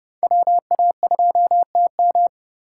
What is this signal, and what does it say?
Morse code, heard as a keyed tone, about 34 wpm, 700 Hz. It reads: WA2TM